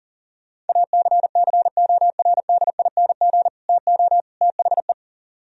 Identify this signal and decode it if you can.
Morse code, heard as a keyed tone, about 36 wpm, 700 Hz. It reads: ACCORDING TO THE
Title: Speed-Racing - Sets of 3 Words 30wpm